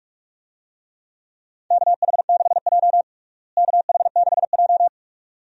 Morse code, heard as a keyed tone, about 46 wpm, 700 Hz. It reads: KH6J KH6J
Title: DXCC-100 Call Signs - 4 Characters - Repeated 2x - 45wpm